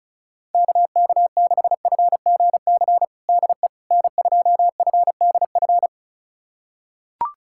Morse code, heard as a keyed tone, about 36 wpm, 690 Hz. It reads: KK6FGC DE N2FDF
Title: USA Callsign DE Callsign 35wpm